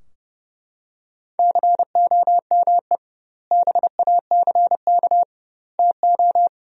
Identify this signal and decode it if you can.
Morse code, heard as a keyed tone, about 30 wpm, 700 Hz. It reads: COME BACK TO
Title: 2x Word-Spacing - Sentences from Top 500 Words - 30wpm